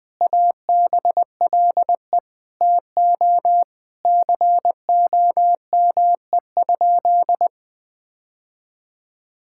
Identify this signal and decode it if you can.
Morse code, heard as a keyed tone, about 20 wpm, 700 Hz. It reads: ABLE TO COME?
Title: Sentences with 5-Word Phrases - 20wpm